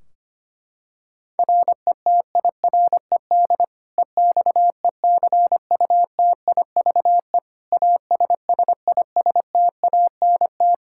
Morse code, heard as a keyed tone, about 25 wpm, 700 Hz. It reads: RETIRED EXECUTIVE ASSISTANT
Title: K1USN SST QSO Practice 25wpm